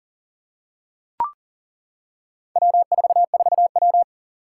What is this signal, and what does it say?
Morse code, heard as a keyed tone, about 40 wpm, 700 Hz. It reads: W44W
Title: Four Challenging Letter-Number Combinations 40wpm